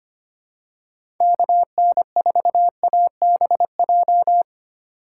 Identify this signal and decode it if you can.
Morse code, heard as a keyed tone, about 25 wpm, 700 Hz. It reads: KN4ABJ
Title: USA Call Signs - 6 Characters - 3x Character Spacing - 25wpm